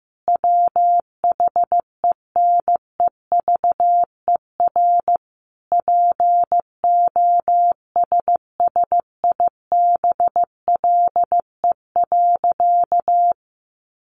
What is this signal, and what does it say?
Morse code, heard as a keyed tone, about 15 wpm, 700 Hz. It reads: WHENEVER POSSIBLE.
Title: Sentences from Top 1200 Words 15wpm